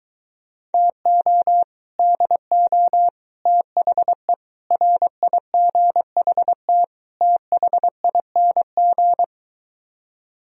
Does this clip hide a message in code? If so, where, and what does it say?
Yes: Morse code from 0.7 to 9.2 s, heard as a keyed tone, about 23 wpm, 700 Hz. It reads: TO DO THE RIGHT THING